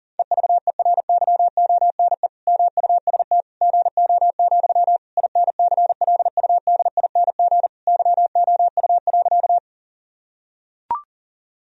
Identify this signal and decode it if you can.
Morse code, heard as a keyed tone, about 40 wpm, 700 Hz. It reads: EVERYONE MUST GO, INCLUDING YOU.
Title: Sentences from Top 1000 Words - Review Part 4 - 40wpm